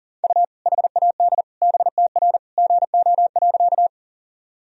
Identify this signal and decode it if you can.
Morse code, heard as a keyed tone, about 40 wpm, 700 Hz. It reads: U HAD BTR GO.
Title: Sentences from Top 400 Words and Abbreviations 40wpm